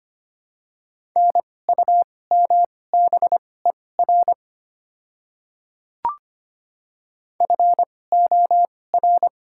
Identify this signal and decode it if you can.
Morse code, heard as a keyed tone, about 25 wpm, spaced as for 18 wpm, 700 Hz. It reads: NUMBER FOR
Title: 2x Character Spacing - Top 100 Words - 25wpm